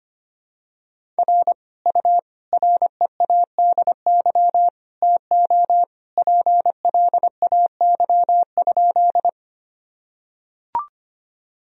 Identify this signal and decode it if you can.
Morse code, heard as a keyed tone, about 25 wpm, 700 Hz. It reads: R U READY TO PLAY?
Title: Sentences from Top 400 Words and Abbreviations 25wpm